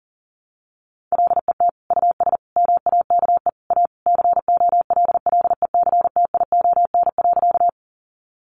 Morse code, heard as a keyed tone, about 40 wpm, 700 Hz. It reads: LET US MAKE A COLLECTION.
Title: Sentences from Top 1200 Words 40wpm